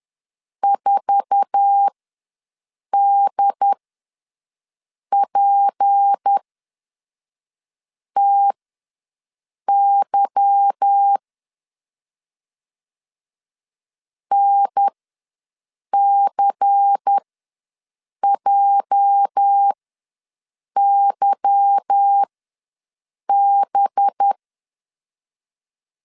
Morse code, heard as a keyed tone, about 10 wpm, spaced as for 5 wpm, 800 Hz. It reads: 4DPTY NCJYB
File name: LEZIONI / 8 YJ45 / 8 YJ4 ws 10wpm 4020